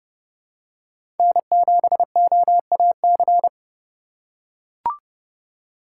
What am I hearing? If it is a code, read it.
Morse code, heard as a keyed tone, about 30 wpm, 700 Hz. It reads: N7OAC
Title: USA Call Signs - 5 Characters - 4x Character Spacing - 30wpm